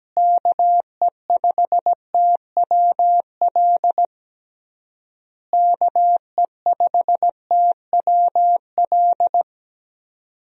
Morse code, heard as a keyed tone, about 17 wpm, 700 Hz. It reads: KE5TWL KE5TWL
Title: USA Call Signs - 6 Characters - Repeated 3x - 17wpm